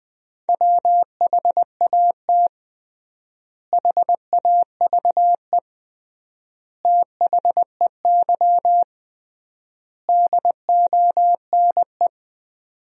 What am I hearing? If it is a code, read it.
Morse code, heard as a keyed tone, about 20 wpm, 700 Hz. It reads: WHAT HAVE THEY DONE
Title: 3x Word-Spacing - Sentences from Top 500 Words - 20wpm